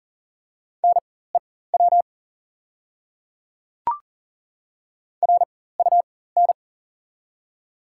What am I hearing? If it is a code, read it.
Morse code, heard as a keyed tone, about 40 wpm, spaced as for 19 wpm, 700 Hz. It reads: NEW RUN
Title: 4x Character Spacing - Top 300 Words - 40wpm